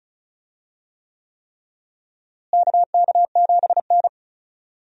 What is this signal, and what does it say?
Morse code, heard as a keyed tone, about 36 wpm, 700 Hz. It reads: KK7N
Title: USA Call Signs - 4 Characters - Repeated 3x - 35wpm